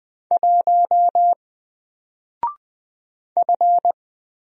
Morse code, heard as a keyed tone, about 20 wpm, 700 Hz. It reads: 1 F
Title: Single Letter-Number Alt 20wpm